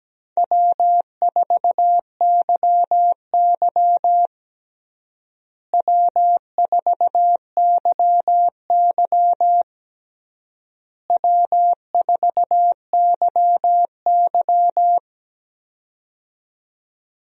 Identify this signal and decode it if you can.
Morse code, heard as a keyed tone, about 17 wpm, 700 Hz. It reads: W4YY W4YY W4YY
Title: USA Call Signs - 4 Characters - Repeated 3x - 17wpm